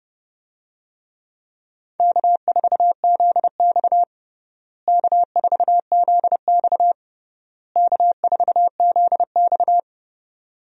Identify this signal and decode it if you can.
Morse code, heard as a keyed tone, about 30 wpm, 700 Hz. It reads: K4ZX K4ZX K4ZX
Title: USA Call Signs - 4 Characters - Repeated 3x - 30wpm